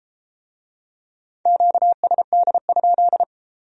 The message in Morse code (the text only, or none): QSD?